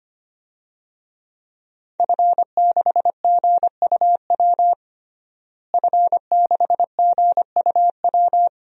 Morse code, heard as a keyed tone, about 25 wpm, 700 Hz. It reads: F6GUW F6GUW
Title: DXCC-100 Call Signs - 5 Characters - Repeated 2x - 25wpm